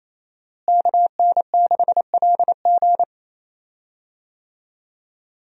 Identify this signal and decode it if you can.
Morse code, heard as a keyed tone, about 28 wpm, 700 Hz. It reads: KN6LG
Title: USA Call Signs - 5 Characters - Repeated 2x - 28wpm